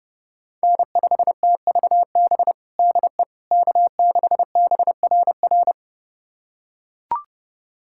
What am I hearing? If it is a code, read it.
Morse code, heard as a keyed tone, about 30 wpm, 700 Hz. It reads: N5TVB DE K6BRR
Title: USA Callsign DE Callsign - 5 Characters - 30wpm